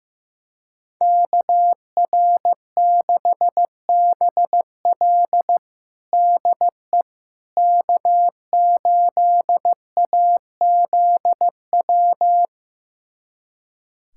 Morse code, heard as a keyed tone, about 15 wpm, 700 Hz. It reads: KR6BL DE K8AZW
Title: USA Callsign DE Callsign - 5 Characters - 15wpm